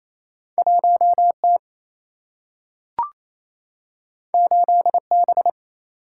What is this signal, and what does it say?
Morse code, heard as a keyed tone, about 28 wpm, 700 Hz. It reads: 1 T 8 B
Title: Single-Letter Single-Number 28wpm